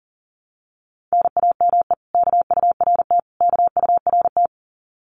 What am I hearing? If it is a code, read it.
Morse code, heard as a keyed tone, about 40 wpm, 700 Hz. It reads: NAME KURT KURT
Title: Common QSO Phrases Encore 40wpm